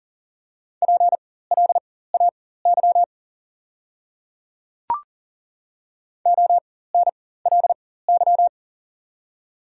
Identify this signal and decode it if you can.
Morse code, heard as a keyed tone, about 37 wpm, spaced as for 18 wpm, 700 Hz. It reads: PLAY ONLY
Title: 4x Character Spacing - Top 200 Words - 40wpm